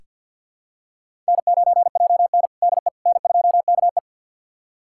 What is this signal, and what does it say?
Morse code, heard as a keyed tone, about 46 wpm, 700 Hz. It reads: N9JN DE N2KE
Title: USA Callsign DE Callsign - 4 Characters - 50wpm